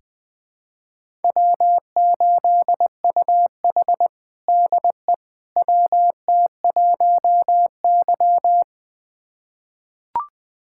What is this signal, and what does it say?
Morse code, heard as a keyed tone, about 20 wpm, 700 Hz. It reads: W8UH DE WT1Y
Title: USA Callsign DE Callsign - 4 Characters - 20wpm